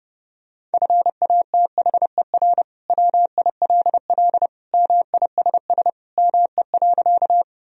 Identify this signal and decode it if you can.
Morse code, heard as a keyed tone, about 30 wpm, 700 Hz. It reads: FATHER WILL MISS ME.
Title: Sentences from Top 1000 Words - Review Part 3 - 30wpm